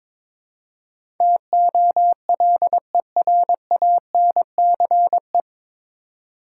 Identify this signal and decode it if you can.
Morse code, heard as a keyed tone, about 22 wpm, 700 Hz. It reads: TOLERANCE
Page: Morse Code Ninja practice 22wpm